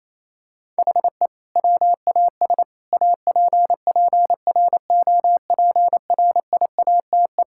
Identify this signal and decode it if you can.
Morse code, heard as a keyed tone, about 28 wpm, 700 Hz. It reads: HE WAS APPROPRIATE
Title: Sentences from Top 1000 Words - Review Part 5 - 28wpm